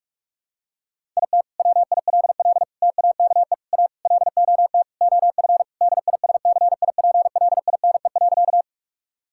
Morse code, heard as a keyed tone, about 45 wpm, 700 Hz. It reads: IT WILL TAKE A LOT OF DISCIPLINE.
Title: Sentences from Top 1000 Words - Review Part 6 - 45wpm